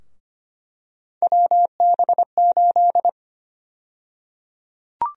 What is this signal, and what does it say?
Morse code, heard as a keyed tone, about 25 wpm, 700 Hz. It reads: WB8
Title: USA Call Sign Prefixes - 25wpm